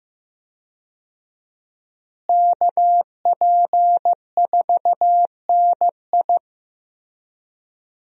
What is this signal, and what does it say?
Morse code, heard as a keyed tone, about 15 wpm, 700 Hz. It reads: KP4NI
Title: USA Call Signs - 5 Characters - Repeated 3x - 15wpm